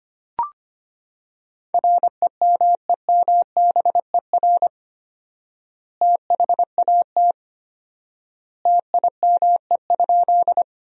Morse code, heard as a keyed tone, about 25 wpm, 700 Hz. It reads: REMEMBER THAT TIME?
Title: 4x Word-Spacing - Sentences from Top 400 Words - 25wpm